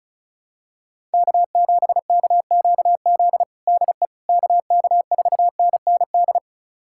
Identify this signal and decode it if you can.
Morse code, heard as a keyed tone, about 36 wpm, 700 Hz. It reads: K7KQZ DE KK4NND
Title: USA Callsign DE Callsign 35wpm